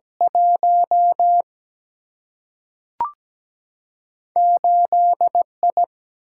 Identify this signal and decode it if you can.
Morse code, heard as a keyed tone, about 17 wpm, 700 Hz. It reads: 1 8I
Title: Single-Letter Single-Number 17wpm